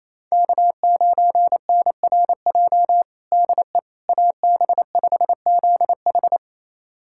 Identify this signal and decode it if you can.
Morse code, heard as a keyed tone, about 28 wpm, 700 Hz. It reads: K9NRJ DE AB5ZH
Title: USA Callsign DE Callsign - 5 Characters - 28wpm